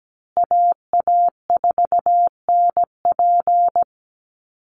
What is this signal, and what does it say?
Morse code, heard as a keyed tone, about 17 wpm, 700 Hz. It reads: AA4NP
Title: Call Signs Repeated 2x - 17wpm